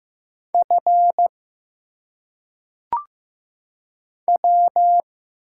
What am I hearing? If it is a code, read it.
Morse code, heard as a keyed tone, about 15 wpm, 700 Hz. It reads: F W
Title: Single Letters - Phonetic Pronunciation 15wpm